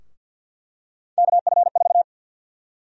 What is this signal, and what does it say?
Morse code, heard as a keyed tone, about 52 wpm, 700 Hz. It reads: KF4
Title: USA Call Sign Prefixes - 50wpm